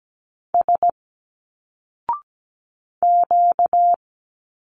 Morse code, heard as a keyed tone, about 17 wpm, 700 Hz. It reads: S Q
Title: Single Letters 17wpm